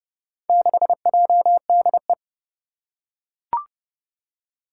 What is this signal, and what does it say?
Morse code, heard as a keyed tone, about 30 wpm, 700 Hz. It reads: 6JDE
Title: Four Challenging Letter-Number Combinations 30wpm